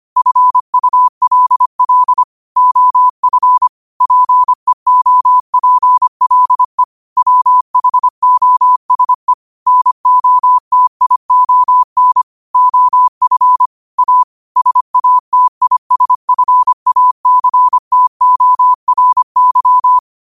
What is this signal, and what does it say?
Morse code, heard as a keyed tone, about 25 wpm, 1000 Hz. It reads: FULL OF PEOPLE WHOSE NOTION OF A SATISFACTORY
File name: qotdfiles / 25wpm